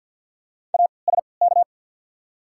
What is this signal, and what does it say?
Morse code, heard as a keyed tone, about 53 wpm, spaced as for 29 wpm, 700 Hz. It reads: ASK
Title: 3x Character Spacing - Top 200 Words - 50wpm